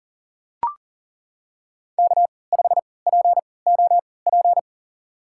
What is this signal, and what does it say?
Morse code, heard as a keyed tone, about 38 wpm, spaced as for 22 wpm, 700 Hz. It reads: K5POP
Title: USA Call Signs - 5 Characters - 3x Character Spacing - 40wpm